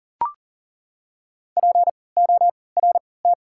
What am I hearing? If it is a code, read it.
Morse code, heard as a keyed tone, about 40 wpm, spaced as for 23 wpm, 700 Hz. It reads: PORT
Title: 3x Character Spacing - Top 200 Words - 40wpm